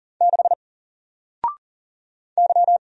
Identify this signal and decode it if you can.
Morse code, heard as a keyed tone, about 40 wpm, 700 Hz. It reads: 6 Y
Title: Single Character (Letter:Number:Punctuation:Prosign) 40wpm